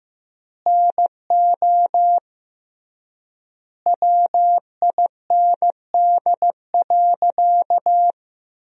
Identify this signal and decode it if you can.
Morse code, heard as a keyed tone, about 15 wpm, 700 Hz. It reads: NO WIND.
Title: 3x Word-Spacing - Sentences from Top 400 Words - 15wpm